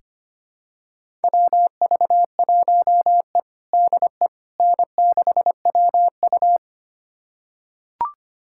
Morse code, heard as a keyed tone, about 25 wpm, 700 Hz. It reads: WV1E DE N6WU
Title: USA Callsign DE Callsign - 4 Characters - 25wpm